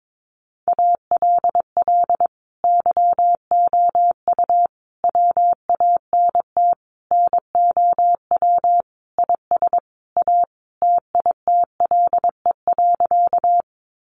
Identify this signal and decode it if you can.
Morse code, heard as a keyed tone, about 22 wpm, 700 Hz. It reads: ALL YOU WANT NOW IS A TITLE.
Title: Sentences from Top 1200 Words 22wpm